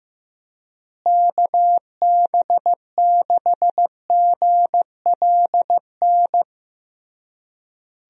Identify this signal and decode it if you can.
Morse code, heard as a keyed tone, about 15 wpm, 700 Hz. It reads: KB6GLN
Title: USA Call Signs - 6 Characters - Repeated 3x - 15wpm